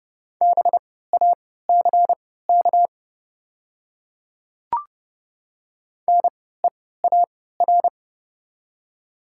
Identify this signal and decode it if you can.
Morse code, heard as a keyed tone, about 31 wpm, spaced as for 17 wpm, 700 Hz. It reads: BACK NEAR
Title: 3x Character Spacing - Top 200 Words - 30wpm